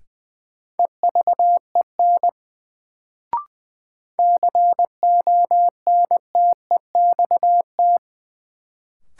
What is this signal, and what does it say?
Morse code, heard as a keyed tone, about 20 wpm, 700 Hz. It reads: EVEN CONTEXT
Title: Sets of 4 Words - New Word Drill - 20wpm